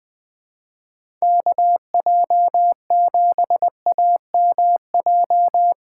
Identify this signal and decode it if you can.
Morse code, heard as a keyed tone, about 20 wpm, 700 Hz. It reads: KJ7AMJ
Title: USA Call Signs - 2x Character Spacing - 20wpm